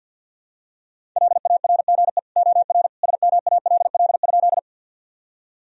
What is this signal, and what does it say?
Morse code, heard as a keyed tone, about 50 wpm, 700 Hz. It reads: LARGE OR SMALL?
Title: Sentences from Top 1000 Words - Review Part 1 - 50wpm